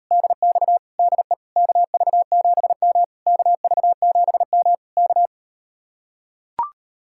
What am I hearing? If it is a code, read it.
Morse code, heard as a keyed tone, about 38 wpm, 700 Hz. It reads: DX DE KV7M KV7M K